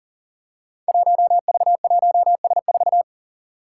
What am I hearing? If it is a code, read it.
Morse code, heard as a keyed tone, about 37 wpm, 700 Hz. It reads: 1V1S4
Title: Five Challenging Letter-Number Combinations 40wpm